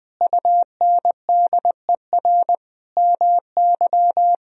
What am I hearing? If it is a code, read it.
Morse code, heard as a keyed tone, about 20 wpm, 700 Hz. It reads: UNDER MY
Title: Sentences from Top 1000 Words - Review Part 1 - 20wpm